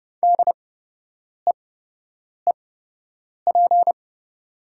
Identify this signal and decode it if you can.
Morse code, heard as a keyed tone, about 32 wpm, spaced as for 8 wpm, 690 Hz. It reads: DEEP
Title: 8x Character Spacing - Top 500 Words - 30wpm